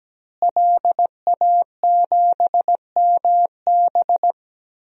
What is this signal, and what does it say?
Morse code, heard as a keyed tone, about 17 wpm, 700 Hz. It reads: LA7MB
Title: DXCC-100 Call Signs - 5 Characters - Repeated 2x - 17wpm